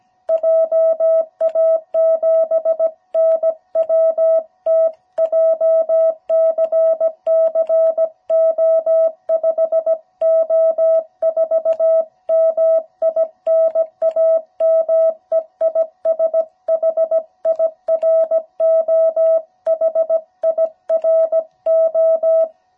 Morse code, heard as a keyed tone, about 17 wpm, 660 Hz. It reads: JA7NWT JCCO5O4 MI NAME IS HIRO HIRO
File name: jh8cbh / 2021cw